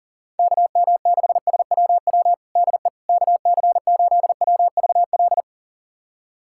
Morse code, heard as a keyed tone, about 40 wpm, 700 Hz. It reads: KM6SWW DE KC8WVL
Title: USA Callsign DE Callsign 40wpm